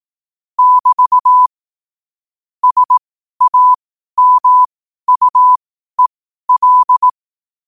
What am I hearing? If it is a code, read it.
Morse code, heard as a keyed tone, about 18 wpm, spaced as for 12 wpm, 1000 Hz. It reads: <BT> SAMUEL
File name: qotdfiles / 12wpm